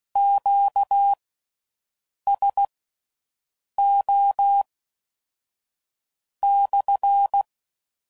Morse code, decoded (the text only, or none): QSO /